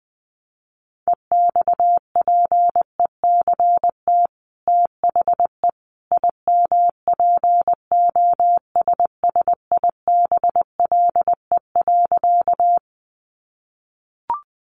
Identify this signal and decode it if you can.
Morse code, heard as a keyed tone, about 20 wpm, 700 Hz. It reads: EXPECT THE IMPOSSIBLE.
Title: Sentences from Top 1100 Words 20wpm